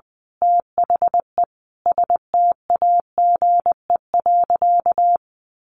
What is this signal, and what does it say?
Morse code, heard as a keyed tone, about 20 wpm, 700 Hz. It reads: THE STAGE.
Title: Sentences from Top 1200 Words 20wpm